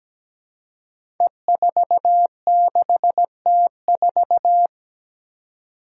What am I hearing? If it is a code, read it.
Morse code, heard as a keyed tone, about 17 wpm, 700 Hz. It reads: E46T4
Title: Five Challenging Letter-Number Combinations 17wpm